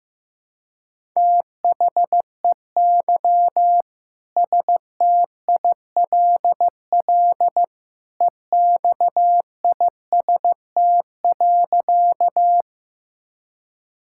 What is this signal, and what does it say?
Morse code, heard as a keyed tone, about 16 wpm, 700 Hz. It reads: THEY STILL EXIST.
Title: Sentences from Top 1000 Words - Review Part 5 - 15wpm